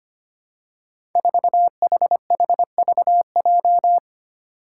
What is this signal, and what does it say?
Morse code, heard as a keyed tone, about 25 wpm, 700 Hz. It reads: 4HHVJ
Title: Five Challenging Letter-Number Combinations 25wpm